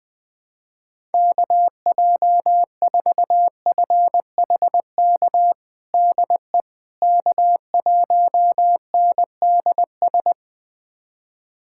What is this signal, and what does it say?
Morse code, heard as a keyed tone, about 20 wpm, 690 Hz. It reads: KJ4FHK DE K1NDS
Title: USA Callsign DE Callsign 20wpm